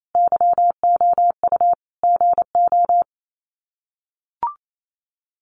Morse code, heard as a keyed tone, about 28 wpm, 700 Hz. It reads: YOU GO
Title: Sets of 5 Words Encore 28wpm